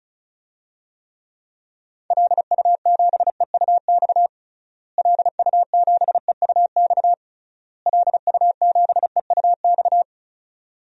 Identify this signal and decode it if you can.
Morse code, heard as a keyed tone, about 34 wpm, 700 Hz. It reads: LU7EUX LU7EUX LU7EUX
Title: DXCC-100 Call Signs - 6 Characters - Repeated 3x - 35wpm